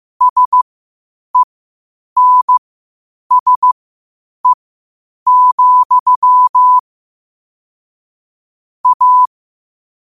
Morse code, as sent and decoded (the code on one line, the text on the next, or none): ... -. ... --..-- / .-
SENSE, A